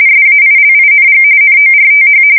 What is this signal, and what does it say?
RTTY, about 45.45 baud. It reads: SAILING NORTH
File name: fsk170 / battleshipnj